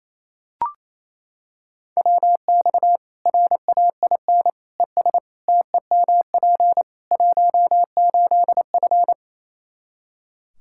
Morse code, heard as a keyed tone, about 28 wpm, 700 Hz. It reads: WX RAIN ES TEMP 18F
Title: K1USN SST QSO Practice 28wpm